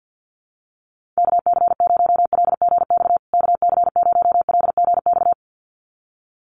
Morse code, heard as a keyed tone, about 55 wpm, 700 Hz. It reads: KC0LGX KC0LGX
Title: USA Call Signs Repeated 2x 50wpm